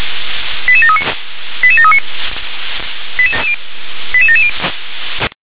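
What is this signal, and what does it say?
Five-tone selective call (ZVEI), heard as quick runs of five tones, tones of about 70 ms. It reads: 88739 88739 88770
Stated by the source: ZVEI five-tone selective call received on an SDR.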